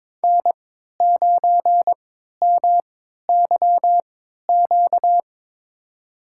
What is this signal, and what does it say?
Morse code, heard as a keyed tone, about 22 wpm, spaced as for 13 wpm, 700 Hz. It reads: N9MYQ
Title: USA Call Signs - 5 Characters - 3x Character Spacing - 22wpm